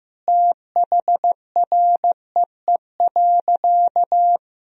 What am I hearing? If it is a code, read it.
Morse code, heard as a keyed tone, about 15 wpm, 700 Hz. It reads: THREE.
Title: Sentences from Top 1000 Words - Review Part 1 - 15wpm